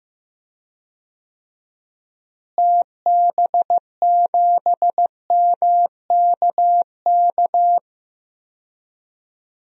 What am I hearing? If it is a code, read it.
Morse code, heard as a keyed tone, about 15 wpm, 700 Hz. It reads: TB7MKK
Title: DXCC-100 Call Signs - 6 Characters - Repeated 2x - 15wpm